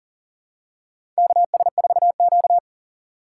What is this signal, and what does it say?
Morse code, heard as a keyed tone, about 40 wpm, 700 Hz. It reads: KS4Q
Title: USA Call Signs - 4 Characters - 4x Character Spacing - 40wpm